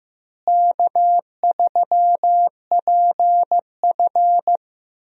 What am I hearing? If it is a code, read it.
Morse code, heard as a keyed tone, about 16 wpm, 700 Hz. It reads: K3PF
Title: USA Call Signs - 4 Characters - Repeated 2x - 15wpm